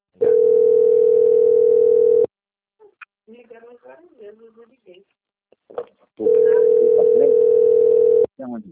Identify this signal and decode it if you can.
Ringback tone (North America), 440 + 480 Hz, with speech.
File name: out- - - - - .14169